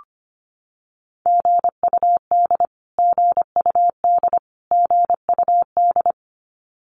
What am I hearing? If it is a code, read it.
Morse code, heard as a keyed tone, about 25 wpm, 700 Hz. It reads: GUD GUD GUD
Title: Abbreviations Repeated 3x - 25wpm